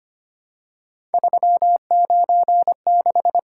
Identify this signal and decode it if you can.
Morse code, heard as a keyed tone, about 25 wpm, 700 Hz. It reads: 396
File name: Three Letter-Number Combinations - 25wpm